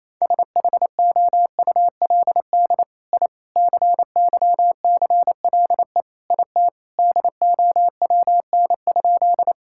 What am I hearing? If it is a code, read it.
Morse code, heard as a keyed tone, about 28 wpm, 700 Hz. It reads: SHOULD I CYCLE IT DOWN?
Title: Sentences from Top 1000 Words - Review Part 6 - 28wpm